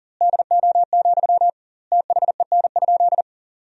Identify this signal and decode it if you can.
Morse code, heard as a keyed tone, about 40 wpm, 700 Hz. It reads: DO, THEN?
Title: 2x Word-Spacing - Sentences from Top 100 Words - 40wpm